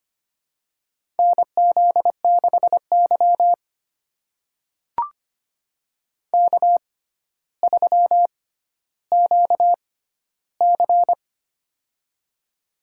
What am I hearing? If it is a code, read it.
Morse code, heard as a keyed tone, about 25 wpm, spaced as for 13 wpm, 700 Hz. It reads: NZ6Y K3QC
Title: USA Call Signs - 4 Characters - 6x Character Spacing - 25wpm